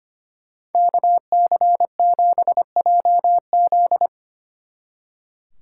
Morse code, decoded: KC7JZ

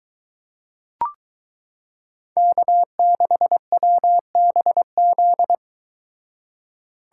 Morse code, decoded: K6WBZ

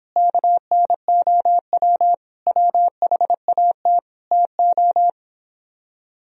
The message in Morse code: KNOW WHAT TO

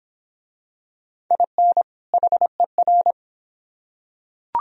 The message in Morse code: IN HER

